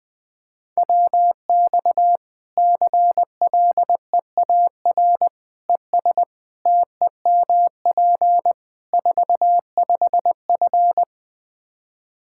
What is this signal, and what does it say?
Morse code, heard as a keyed tone, about 20 wpm, 700 Hz. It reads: WX CLEAR ES TEMP 45F
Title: Practice QSO - Exchange 2 - Info - 20wpm